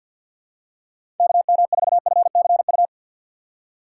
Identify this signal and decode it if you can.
Morse code, heard as a keyed tone, about 54 wpm, 700 Hz. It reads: KM4FCU
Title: USA Call Signs - 4x Character Spacing - 50wpm